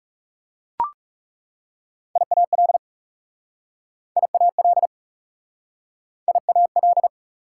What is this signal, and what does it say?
Morse code, heard as a keyed tone, about 40 wpm, 690 Hz. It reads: IAL IAL IAL